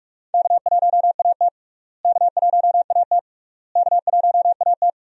Morse code, heard as a keyed tone, about 46 wpm, 700 Hz. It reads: K1AT K1AT K1AT
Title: USA Call Signs - 4 Characters - Repeated 3x - 45wpm